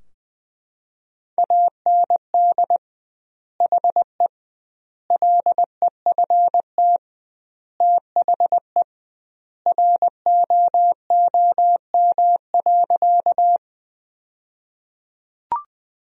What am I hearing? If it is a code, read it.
Morse code, heard as a keyed tone, about 20 wpm, 700 Hz. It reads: AND HE LEFT THE ROOM.
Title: 2x Word-Spacing - Sentences from Top 300 Words - 20wpm